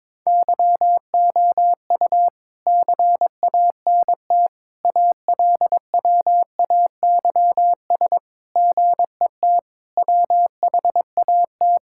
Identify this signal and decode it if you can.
Morse code, heard as a keyed tone, about 22 wpm, 700 Hz. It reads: YOU CANT ALWAYS GET WHAT
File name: Sentences with 7-Word Phrases - 22wpm